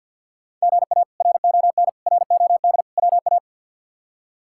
Morse code, heard as a keyed tone, about 49 wpm, 700 Hz. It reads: GA RON ROD WA
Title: K1USN SST QSO Practice - 50wpm